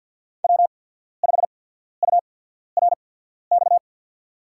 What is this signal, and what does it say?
Morse code, heard as a keyed tone, about 53 wpm, spaced as for 14 wpm, 700 Hz. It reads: W5URX
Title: USA Call Signs - 5 Characters - 8x Character Spacing - 50wpm